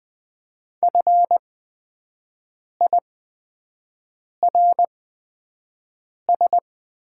Morse code, heard as a keyed tone, about 20 wpm, spaced as for 5 wpm, 700 Hz. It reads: FIRS
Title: 8x Character Spacing - Top 100 Words - 20wpm